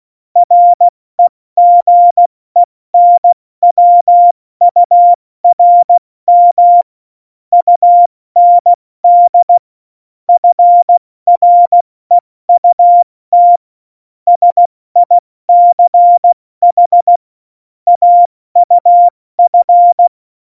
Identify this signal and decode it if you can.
Morse code, heard as a keyed tone, about 16 wpm, 700 Hz. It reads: REGENWURM UND FREUT SICH AUF
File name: Morsekurs / Witze / Witz 14. August 2018, 70 F80